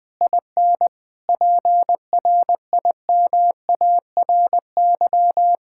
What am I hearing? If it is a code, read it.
Morse code, heard as a keyed tone, about 20 wpm, 700 Hz. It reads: IN PRIMARY
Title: Sentences from Top 1000 Words - Review Part 5 - 20wpm